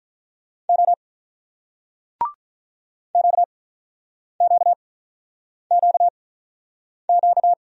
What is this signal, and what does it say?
Morse code, heard as a keyed tone, about 47 wpm, spaced as for 9 wpm, 700 Hz. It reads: O QQQQ